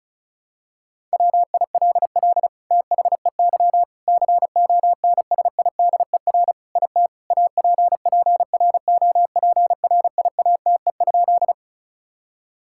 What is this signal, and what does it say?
Morse code, heard as a keyed tone, about 36 wpm, 700 Hz. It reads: WILL THEY CONSIDER IT APPROPRIATE?
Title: Sentences from Top 1000 Words - Review Part 5 - 35wpm